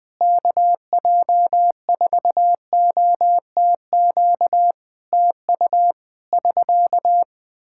Morse code, heard as a keyed tone, about 20 wpm, 700 Hz. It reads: KJ4OTQ TU <SK>
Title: Practice QSO - Exchange 4 - Ending - 20wpm